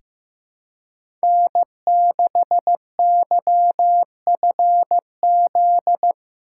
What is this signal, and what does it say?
Morse code, heard as a keyed tone, about 15 wpm, 700 Hz. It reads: N6YFZ
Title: USA Call Signs - 5 Characters - 2x Character Spacing - 15wpm